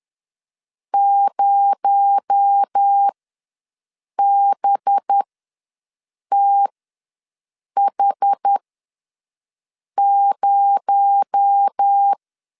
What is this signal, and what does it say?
Morse code, heard as a keyed tone, about 10 wpm, spaced as for 5 wpm, 800 Hz. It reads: 0BTH0
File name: LEZIONI / 6 PKX0 / 6 PKX0 ws 10wpm 4220